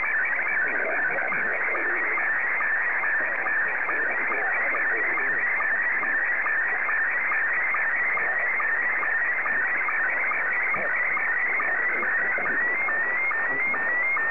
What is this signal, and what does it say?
SSTV (slow-scan television), Scottie 1, with speech.